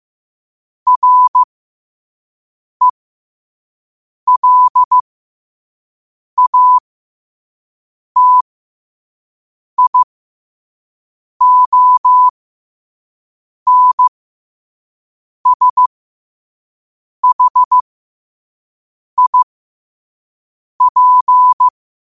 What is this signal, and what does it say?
Morse code, heard as a keyed tone, about 15 wpm, spaced as for 5 wpm, 1000 Hz. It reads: RELATIONSHIP
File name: qotdfiles / 5wpm